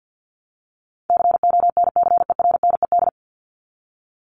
Morse code, heard as a keyed tone, about 49 wpm, 700 Hz. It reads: CONCERNED
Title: ED Suffix - Top 100 Words with Base Words 50wpm